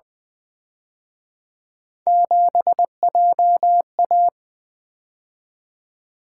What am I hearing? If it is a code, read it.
Morse code, heard as a keyed tone, about 25 wpm, 700 Hz. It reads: 7JA
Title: Speed-Racing - Three Letter-Number Combinations - 20wpm